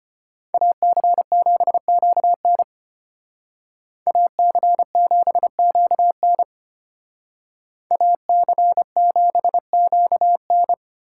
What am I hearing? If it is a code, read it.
Morse code, heard as a keyed tone, about 32 wpm, 700 Hz. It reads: AC7QN AC7QN AC7QN